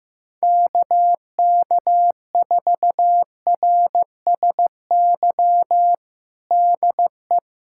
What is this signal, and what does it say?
Morse code, heard as a keyed tone, about 15 wpm, 700 Hz. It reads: KK4RSY DE
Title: USA Callsign DE Callsign - 6 Characters - 15wpm